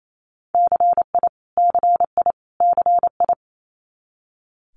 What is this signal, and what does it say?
Morse code, heard as a keyed tone, about 28 wpm, 700 Hz. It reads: CI CI CI
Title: Multi-letter Phonemes Repeated 3x 28wpm